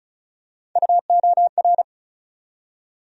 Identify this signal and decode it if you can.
Morse code, heard as a keyed tone, about 36 wpm, 700 Hz. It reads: UOR
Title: USA Call Sign Postfixes - 3 Letters 35wpm